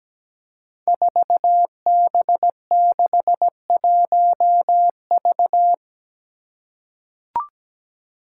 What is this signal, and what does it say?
Morse code, heard as a keyed tone, about 17 wpm, 700 Hz. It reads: 4B61V
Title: Five Challenging Letter-Number Combinations 17wpm